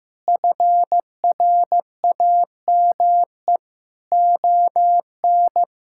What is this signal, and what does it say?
Morse code, heard as a keyed tone, about 16 wpm, 700 Hz. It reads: FRAME ON
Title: Sentences from Top 1000 Words - Review Part 6 - 15wpm